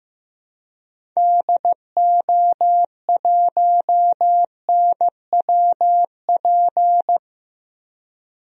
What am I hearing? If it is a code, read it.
Morse code, heard as a keyed tone, about 15 wpm, 700 Hz. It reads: DO1NWP
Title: DXCC-100 Call Signs - 6 Characters - 15wpm